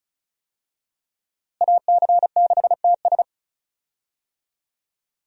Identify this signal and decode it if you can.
Morse code, heard as a keyed tone, about 34 wpm, 700 Hz. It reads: AC6TS
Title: USA Call Signs - 5 Characters - Repeated 2x - 35wpm